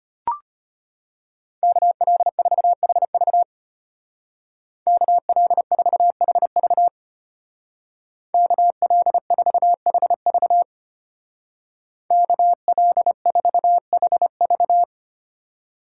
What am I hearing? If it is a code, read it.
Morse code, heard as a keyed tone, about 32 wpm, 700 Hz. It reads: KL4HV KL4HV KL4HV KL4HV